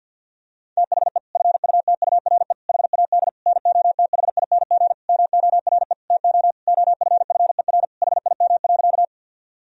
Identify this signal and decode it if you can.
Morse code, heard as a keyed tone, about 49 wpm, 700 Hz. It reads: THE FUTURE HAD NOTHING MORE TO OFFER HIM.
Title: Sentences from Top 1000 Words - Review Part 4 - 50wpm